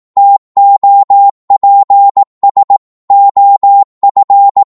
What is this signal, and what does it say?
Morse code, heard as a keyed tone, about 18 wpm, 800 Hz. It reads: TOPS OF
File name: 18wpm